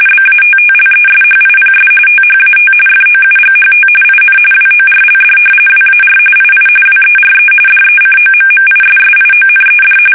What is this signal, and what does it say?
RTTY, about 45.45 baud. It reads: BATTLESHIPS THAT FOUGHT IN WORLD WAR II WERE DECOMMISSIONED
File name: fsk850 / battleshipnj